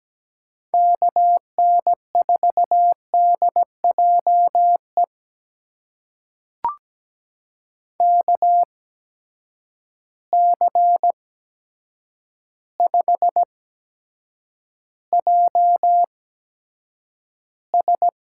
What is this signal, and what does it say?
Morse code, heard as a keyed tone, about 17 wpm, spaced as for 8 wpm, 700 Hz. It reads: KN4DJE KC5JS
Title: USA Call Signs - 6 Characters - 8x Character Spacing - 17wpm